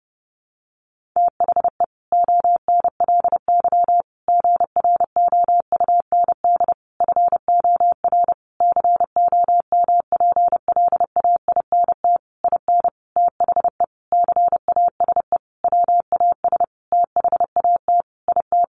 Morse code, heard as a keyed tone, about 30 wpm, 700 Hz. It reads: THE ONLY GROUND FOR COMPLAINT IN THE CASE WAS THAT IT
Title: Sentences from Top 1400 Words 30wpm